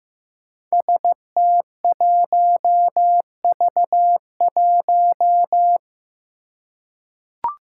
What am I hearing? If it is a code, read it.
Morse code, heard as a keyed tone, about 15 wpm, 700 Hz. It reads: ST1V1
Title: Five Challenging Letter-Number Combinations 15wpm